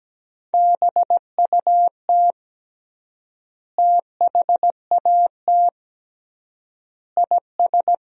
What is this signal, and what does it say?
Morse code, heard as a keyed tone, about 17 wpm, 700 Hz. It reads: BUT THAT IS
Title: 3x Word-Spacing - Sentences from Top 500 Words - 17wpm